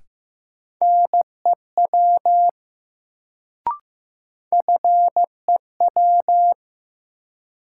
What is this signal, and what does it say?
Morse code, heard as a keyed tone, about 15 wpm, 700 Hz. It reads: NEW FEW
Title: Sets of 2 Words - New Word Drill - 15wpm